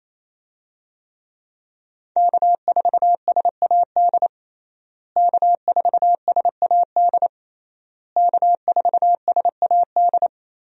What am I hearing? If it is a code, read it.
Morse code, heard as a keyed tone, about 28 wpm, 700 Hz. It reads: K4SAD K4SAD K4SAD
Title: USA Call Signs - 5 Characters - Repeated 3x - 28wpm